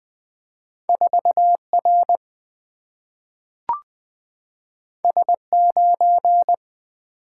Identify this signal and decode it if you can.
Morse code, heard as a keyed tone, about 20 wpm, 690 Hz. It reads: 4R S9
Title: Two Letter-Number Combinations - 20wpm